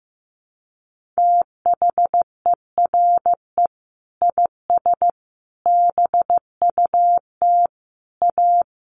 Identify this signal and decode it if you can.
Morse code, heard as a keyed tone, about 15 wpm, 700 Hz. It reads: THERE IS BUT A
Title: Sentences from Top 1400 Words 15wpm